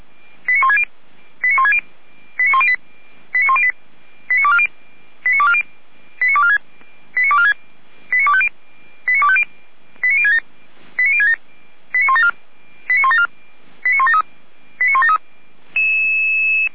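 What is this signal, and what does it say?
Five-tone selective call (ZVEI), heard as quick runs of five tones, tones of about 80 ms. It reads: 89160 89108 89108 89240 89246 89246 89250 89067 89067 89174 89173 89173
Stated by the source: Five-tone ZVEI selective call received on SDR.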